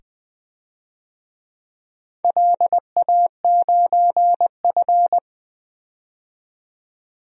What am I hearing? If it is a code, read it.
Morse code, heard as a keyed tone, about 20 wpm, 700 Hz. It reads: LA9F
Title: DXCC-100 Call Signs - 4 Characters - 20wpm